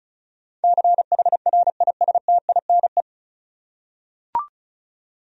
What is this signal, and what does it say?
Morse code, heard as a keyed tone, about 36 wpm, 700 Hz. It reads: CHRISTINE
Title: USA Common Names - 35wpm